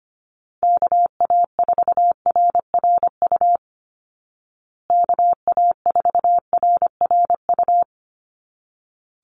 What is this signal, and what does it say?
Morse code, heard as a keyed tone, about 25 wpm, 700 Hz. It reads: KA4RRU KA4RRU
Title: Call Signs Repeated 3x - 25wpm